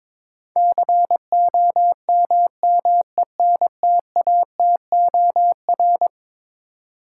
Morse code, heard as a keyed tone, about 22 wpm, 700 Hz. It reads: COMMENTATOR
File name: practice / 22wpm / Eleven Letter Words - 22wpm